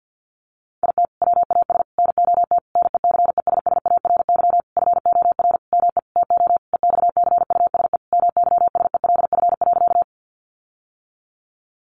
Morse code, heard as a keyed tone, about 50 wpm, 700 Hz. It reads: IT WAS NOT NECESSARY FOR ME TO EXCUSE MYSELF.
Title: Sentences from Top 1300 Words 50wpm